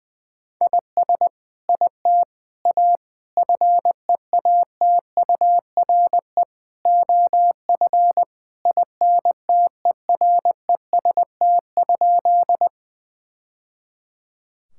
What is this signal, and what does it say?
Morse code, heard as a keyed tone, about 20 wpm, 700 Hz. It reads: IS IT A FEATURE OF INTEREST?
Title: Sentences from Top 1000 Words - Review Part 5 - 20wpm